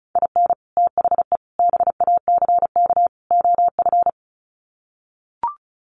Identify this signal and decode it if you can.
Morse code, heard as a keyed tone, about 34 wpm, 700 Hz. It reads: IN THE BACK OF
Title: Sets of 4 Words 35wpm